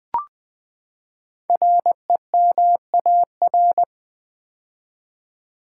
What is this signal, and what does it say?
Morse code, heard as a keyed tone, about 20 wpm, 700 Hz. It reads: REMAR
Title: Build A Word - 5 Letters At A Time - 10-Letter Words - 20wpm